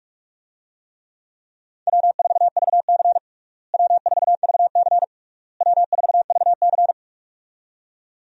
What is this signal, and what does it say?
Morse code, heard as a keyed tone, about 46 wpm, 700 Hz. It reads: W4VC W4VC W4VC
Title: USA Call Signs - 4 Characters - Repeated 3x - 45wpm